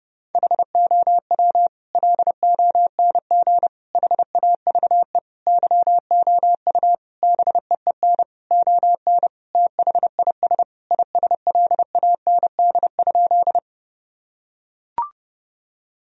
Morse code, heard as a keyed tone, about 31 wpm, 700 Hz. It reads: HOW LONG HAVE YOU BEEN ON THIS ISLAND?
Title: Sentences from Top 1000 Words - Review Part 3 - 30wpm